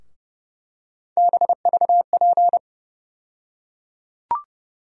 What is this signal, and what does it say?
Morse code, heard as a keyed tone, about 30 wpm, 700 Hz. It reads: BVP
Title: USA Call Sign Postfixes - 3 Letters 30wpm